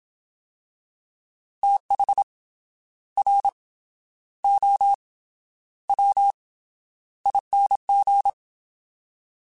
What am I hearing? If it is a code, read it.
Morse code, heard as a keyed tone, about 27 wpm, 800 Hz. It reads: TH R O W ING
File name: TH / TH mp3s / Fast TH @25